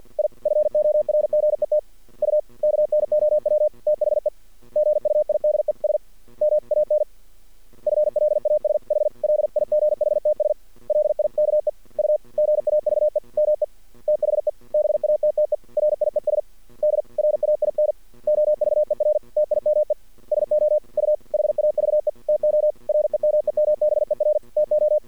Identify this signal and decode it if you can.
Morse code, heard as a keyed tone, about 48 wpm, 610 Hz. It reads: I FORGET U KNOW THE GUIDES MAN FLASH LIGHTS LIKE U GAVE ME THE BATTERIES DRAIN OVER TIME SO U HAVE TO REMEMBER TO
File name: archived / CRSnet